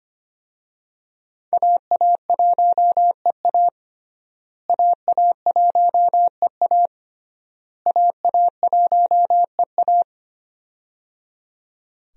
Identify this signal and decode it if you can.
Morse code, heard as a keyed tone, about 25 wpm, 700 Hz. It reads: AA1EA AA1EA AA1EA
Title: USA Call Signs - 5 Characters - Repeated 3x - 25wpm